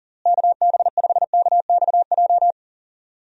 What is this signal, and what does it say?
Morse code, heard as a keyed tone, about 40 wpm, 700 Hz. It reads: KB5KXJ
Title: USA Call Signs - 6 Characters - Repeated 3x - 40wpm